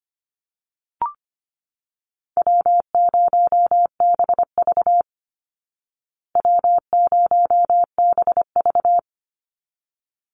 Morse code, heard as a keyed tone, about 25 wpm, 700 Hz. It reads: W0BV W0BV
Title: Easy Call Signs Repeated 2x - 25wpm